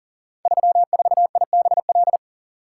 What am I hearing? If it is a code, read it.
Morse code, heard as a keyed tone, about 39 wpm, 700 Hz. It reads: 34IBL
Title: Code Groups - 5 Letter:Numbers - 40wpm